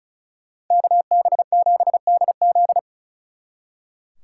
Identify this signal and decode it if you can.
Morse code, heard as a keyed tone, about 47 wpm, spaced as for 41 wpm, 700 Hz. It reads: KB7DZ
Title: Speed Racing - USA 2x2 Call Signs 35wpm